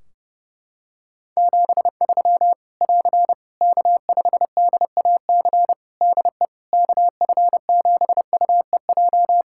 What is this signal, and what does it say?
Morse code, heard as a keyed tone, about 30 wpm, 700 Hz. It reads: 73 <AR> K5DAC DE KF7UEJ
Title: Practice QSO - Exchange 4 - Ending - 30wpm